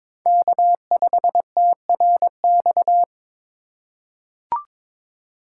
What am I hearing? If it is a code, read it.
Morse code, heard as a keyed tone, about 22 wpm, 700 Hz. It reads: K5TRX
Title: USA Call Signs - 5 Characters - 2x Character Spacing - 22wpm